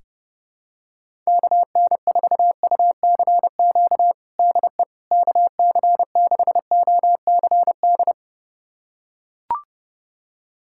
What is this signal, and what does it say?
Morse code, heard as a keyed tone, about 30 wpm, 700 Hz. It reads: KN4UCQ DE KC6OCD
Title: USA Callsign DE Callsign - 6 Characters - 30wpm